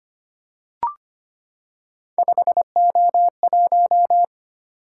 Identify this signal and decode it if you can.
Morse code, heard as a keyed tone, about 25 wpm, 700 Hz. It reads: 5O1
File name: Three Letter-Number Combinations - 25wpm